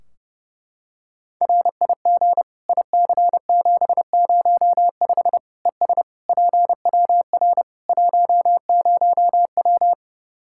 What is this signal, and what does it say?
Morse code, heard as a keyed tone, about 30 wpm, 700 Hz. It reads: RIG IC705 ES PWR 10W
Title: Practice QSO - Exchange 2 - Info - 30wpm